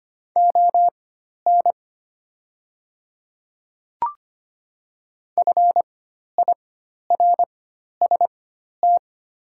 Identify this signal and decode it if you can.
Morse code, heard as a keyed tone, about 25 wpm, spaced as for 12 wpm, 700 Hz. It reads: ON FIRST
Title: 4x Character Spacing - Top 100 Words - 25wpm